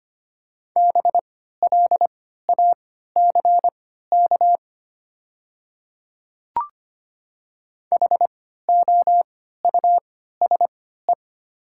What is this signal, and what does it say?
Morse code, heard as a keyed tone, about 25 wpm, spaced as for 14 wpm, 700 Hz. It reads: BLACK HOUSE
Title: 3x Character Spacing - Top 500 Words - 25wpm